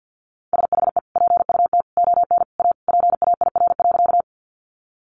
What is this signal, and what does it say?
Morse code, heard as a keyed tone, about 50 wpm, 700 Hz. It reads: SHE PUT ON A PAIR.
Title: Sentences from Top 1200 Words 50wpm